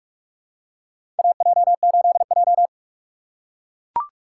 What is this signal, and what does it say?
Morse code, heard as a keyed tone, about 44 wpm, 700 Hz. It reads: AJ8J
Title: USA Call Signs - 4 Characters - 6x Character Spacing - 45wpm